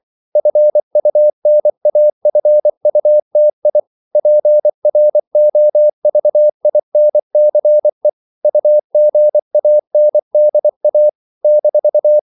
Morse code, heard as a keyed tone, about 24 wpm, 600 Hz. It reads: FUNAFUTI PROVINCE UGANDA -
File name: CW / Practice / NationalCapitals24